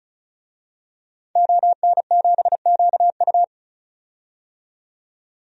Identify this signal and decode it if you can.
Morse code, heard as a keyed tone, about 36 wpm, 700 Hz. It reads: ON7QU
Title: DXCC-100 Call Signs - 5 Characters - 35wpm